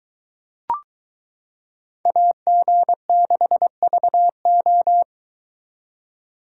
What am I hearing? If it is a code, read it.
Morse code, heard as a keyed tone, about 22 wpm, 700 Hz. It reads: AG6VO